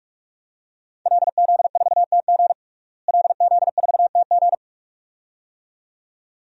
Morse code, heard as a keyed tone, about 46 wpm, 700 Hz. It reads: LZ4TG LZ4TG
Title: DXCC-100 Call Signs - 5 Characters - Repeated 2x - 45wpm